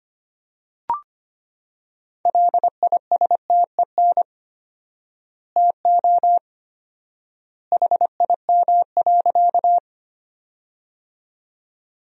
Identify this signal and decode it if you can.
Morse code, heard as a keyed tone, about 25 wpm, 700 Hz. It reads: LISTEN TO HIM.
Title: 4x Word-Spacing - Sentences from Top 400 Words - 25wpm